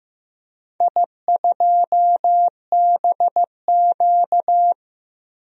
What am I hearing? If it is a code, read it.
Morse code, heard as a keyed tone, about 16 wpm, 700 Hz. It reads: I2BQ
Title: DXCC-100 Call Signs - 4 Characters - Repeated 3x - 15wpm